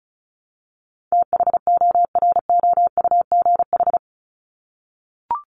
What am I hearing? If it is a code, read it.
Morse code, heard as a keyed tone, about 34 wpm, 700 Hz. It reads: THOROUGH
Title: LY Suffix - Top 100 Words with Base Words 35wpm